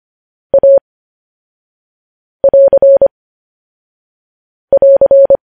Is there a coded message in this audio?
Yes: Morse code from 0.5 to 5.4 s, heard as a keyed tone, about 25 wpm, 550 Hz. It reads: A <AR> <AR>